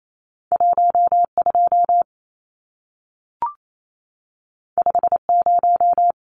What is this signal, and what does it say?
Morse code, heard as a keyed tone, about 28 wpm, 700 Hz. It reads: 12 50
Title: Numbers - 2-Digit 28wpm